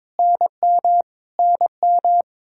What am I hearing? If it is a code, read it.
Morse code, heard as a keyed tone, about 22 wpm, 700 Hz. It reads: NM NM
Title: K1USN SST QSO Practice 22wpm